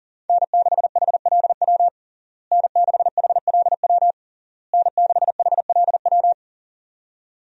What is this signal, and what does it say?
Morse code, heard as a keyed tone, about 40 wpm, 700 Hz. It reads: N6HLW N6HLW N6HLW
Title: USA Call Signs - 5 Characters - Repeated 3x - 40wpm